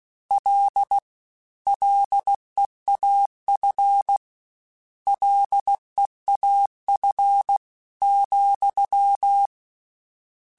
Morse code, heard as a keyed tone, about 16 wpm, 800 Hz. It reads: L LEAF LEAF,
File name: Combinations / Combinations-mp3s / Combinations@15